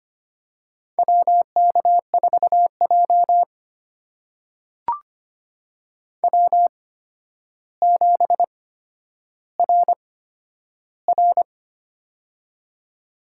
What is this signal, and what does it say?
Morse code, heard as a keyed tone, about 25 wpm, spaced as for 11 wpm, 700 Hz. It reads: WK4J W7RR
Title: USA Call Signs - 4 Characters - 8x Character Spacing - 25wpm